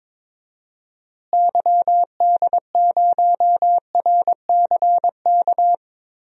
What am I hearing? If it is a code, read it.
Morse code, heard as a keyed tone, about 22 wpm, 700 Hz. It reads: YD0RCK
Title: DXCC-100 Call Signs - 6 Characters - Repeated 3x - 22wpm